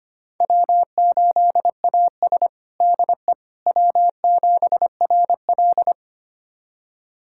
Morse code, heard as a keyed tone, about 25 wpm, 700 Hz. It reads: W8AS DE W7RL
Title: USA Callsign DE Callsign - 4 Characters - 25wpm